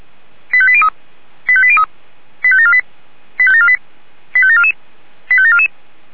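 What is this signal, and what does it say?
Five-tone selective call (ZVEI), heard as quick runs of five tones, tones of about 70 ms. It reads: 85803 85803 85748 85748 85744 85744
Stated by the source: ZVEI five-tone call received by SDR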